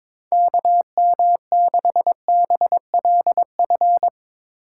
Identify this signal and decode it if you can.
Morse code, heard as a keyed tone, about 24 wpm, 700 Hz. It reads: KM6BLF
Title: Speed Racing - USA 2x3 Call Signs 22wpm